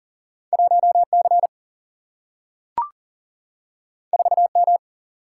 Morse code, heard as a keyed tone, about 39 wpm, 700 Hz. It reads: 1 C 4 M
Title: Single-Letter Single-Number 40wpm